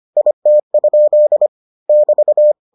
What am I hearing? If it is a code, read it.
Morse code, heard as a keyed tone, about 25 wpm, 600 Hz. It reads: IT? <BT>